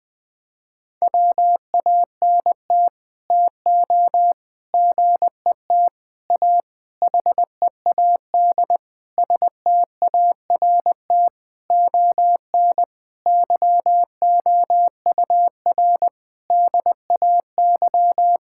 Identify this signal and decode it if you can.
Morse code, heard as a keyed tone, about 19 wpm, 700 Hz. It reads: WANT TO GET A HEAD START ON YOUR DAY
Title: Sets of 9 Words - 20wpm